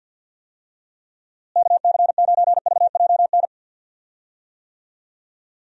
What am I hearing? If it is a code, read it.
Morse code, heard as a keyed tone, about 52 wpm, 700 Hz. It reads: KC9VJN
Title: USA Call Signs - 6 Characters - Repeated 2x - 50wpm